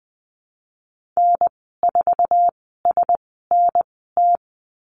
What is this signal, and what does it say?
Morse code, heard as a keyed tone, about 20 wpm, spaced as for 15 wpm, 700 Hz. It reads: N4SNT
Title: 048 Lesson 9g - Number 4 - Call Signs 20wpm